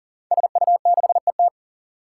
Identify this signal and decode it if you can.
Morse code, heard as a keyed tone, about 40 wpm, 700 Hz. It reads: SU6ET
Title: Five Challenging Letter-Number Combinations 40wpm